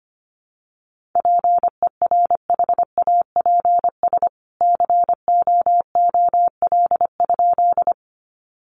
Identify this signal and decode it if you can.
Morse code, heard as a keyed tone, about 25 wpm, 700 Hz. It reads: PERHAPS COOL?
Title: Sentences from Top 1200 Words 25wpm